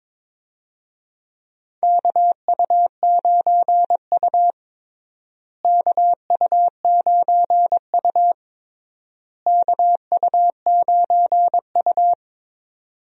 Morse code, heard as a keyed tone, about 22 wpm, 700 Hz. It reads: KU9U KU9U KU9U
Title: USA Call Signs - 4 Characters - Repeated 3x - 22wpm